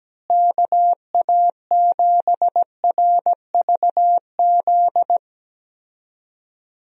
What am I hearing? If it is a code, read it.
Morse code, heard as a keyed tone, about 17 wpm, 700 Hz. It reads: KA7RVZ
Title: USA Call Signs - 6 Characters - Repeated 2x - 17wpm